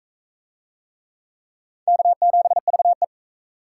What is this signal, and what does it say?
Morse code, heard as a keyed tone, about 38 wpm, 700 Hz. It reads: K7VE